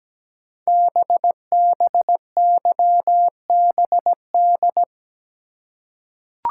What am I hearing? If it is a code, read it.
Morse code, heard as a keyed tone, about 17 wpm, 700 Hz. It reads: BBYBD
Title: Code Groups - 5 Letters - 17wpm